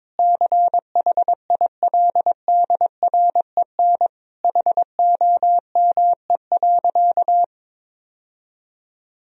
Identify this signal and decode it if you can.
Morse code, heard as a keyed tone, about 22 wpm, 700 Hz. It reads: CHILDREN HOME.
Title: Sentences from Top 1000 Words - Review Part 3 - 22wpm